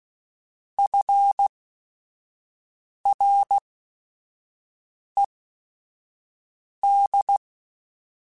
Morse code, heard as a keyed tone, about 16 wpm, spaced as for 5 wpm, 800 Hz. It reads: FRED
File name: F Runs / F Runs mp3s / F Runs @5